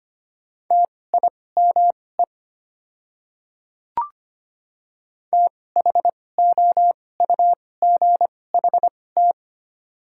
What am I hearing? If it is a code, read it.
Morse code, heard as a keyed tone, about 25 wpm, spaced as for 18 wpm, 700 Hz. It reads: TIME THOUGHT
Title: 2x Character Spacing - Top 500 Words - 25wpm